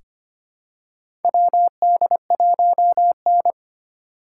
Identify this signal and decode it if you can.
Morse code, heard as a keyed tone, about 25 wpm, 700 Hz. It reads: WD1N
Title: USA Call Signs - 4 Characters - 3x Character Spacing - 25wpm